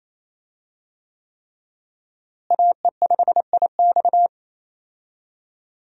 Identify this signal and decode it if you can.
Morse code, heard as a keyed tone, about 28 wpm, 700 Hz. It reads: AE5IX
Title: USA Call Signs - 5 Characters - Repeated 3x - 28wpm